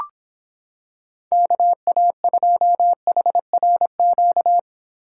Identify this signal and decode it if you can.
Morse code, heard as a keyed tone, about 26 wpm, 700 Hz. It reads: KA2HRQ